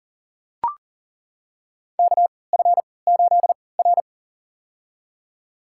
Morse code, heard as a keyed tone, about 40 wpm, spaced as for 23 wpm, 700 Hz. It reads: KF8R
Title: USA Call Signs - 4 Characters - 3x Character Spacing - 40wpm